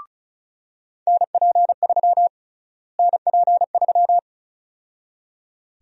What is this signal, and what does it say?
Morse code, heard as a keyed tone, about 34 wpm, 700 Hz. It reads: NP3 NP3
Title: USA Call Sign Prefixes - Repeated 2x - 35wpm